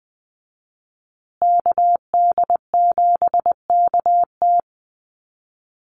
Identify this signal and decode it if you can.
Morse code, heard as a keyed tone, about 20 wpm, 700 Hz. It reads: KD7KT
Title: USA Call Signs Repeated 2x 20wpm